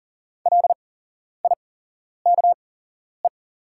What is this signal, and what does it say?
Morse code, heard as a keyed tone, about 38 wpm, spaced as for 11 wpm, 700 Hz. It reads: LIKE
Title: 8x Character Spacing - Top 300 Words - 40wpm